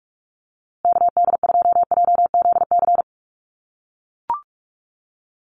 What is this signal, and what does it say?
Morse code, heard as a keyed tone, about 44 wpm, 700 Hz. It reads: KD2JZC
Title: USA Call Signs Repeated 2x 45wpm